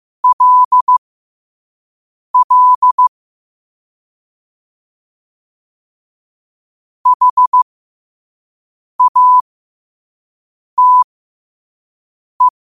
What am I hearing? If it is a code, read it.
Morse code, heard as a keyed tone, about 15 wpm, spaced as for 5 wpm, 1000 Hz. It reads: LL HATE
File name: qotdfiles / 5wpm